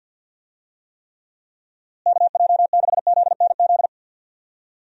Morse code, heard as a keyed tone, about 50 wpm, 700 Hz. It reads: KJ6ZNZ
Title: USA Call Signs - 6 Characters - Repeated 3x - 50wpm